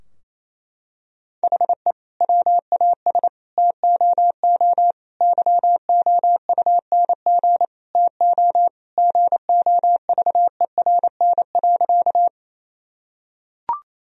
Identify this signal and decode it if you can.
Morse code, heard as a keyed tone, about 28 wpm, 700 Hz. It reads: HE WAS TOO YOUNG TO GOVERN.
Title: Sentences from Top 1000 Words - Review Part 2 - 28wpm